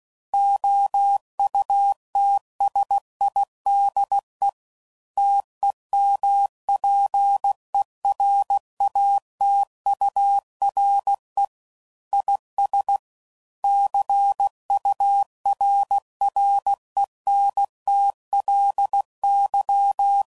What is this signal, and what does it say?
Morse code, heard as a keyed tone, about 16 wpm, 800 Hz. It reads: OUTSIDE TEMPERATURE IS CURRENTLY
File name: CW Code Groups @15WPM / Weather Sets / Weather 5 5